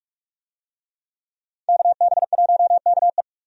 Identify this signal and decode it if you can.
Morse code, heard as a keyed tone, about 47 wpm, 700 Hz. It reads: KB1KE
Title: USA Call Signs - 5 Characters - 45wpm